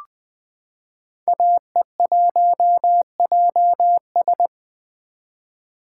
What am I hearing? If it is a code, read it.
Morse code, heard as a keyed tone, about 20 wpm, 700 Hz. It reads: AE1JS
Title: 2023 SOTA Chaser Call Signs - 20wpm